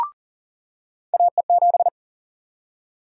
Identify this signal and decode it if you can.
Morse code, heard as a keyed tone, about 40 wpm, 700 Hz. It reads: AE7
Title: USA Call Sign Prefixes - 40wpm